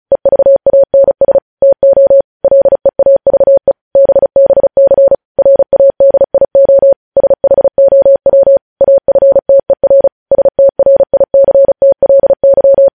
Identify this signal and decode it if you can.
Morse code, heard as a keyed tone, about 36 wpm, 550 Hz. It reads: EVANS TO LEAVE BBC RADIO SHOW AFTER STRICTLY
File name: podcast / News headlines-1x-35x35